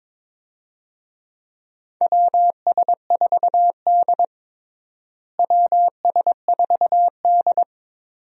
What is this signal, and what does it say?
Morse code, heard as a keyed tone, about 22 wpm, 700 Hz. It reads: WS4D WS4D
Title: USA Call Signs - 4 Characters - Repeated 2x - 22wpm